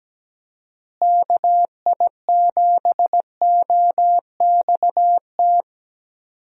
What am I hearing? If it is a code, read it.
Morse code, heard as a keyed tone, about 17 wpm, 700 Hz. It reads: KI7OXT
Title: USA Call Signs - 6 Characters - Repeated 2x - 17wpm